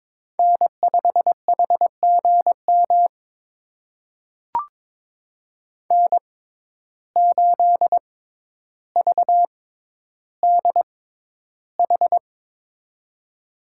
Morse code, heard as a keyed tone, about 22 wpm, spaced as for 11 wpm, 700 Hz. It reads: N5HGM N8VDH
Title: USA Call Signs - 5 Characters - 6x Character Spacing - 22wpm